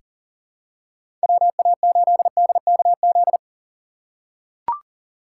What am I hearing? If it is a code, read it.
Morse code, heard as a keyed tone, about 40 wpm, 700 Hz. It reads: WA8DKZ
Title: USA Call Signs - 6 Characters - 6x Character Spacing - 40wpm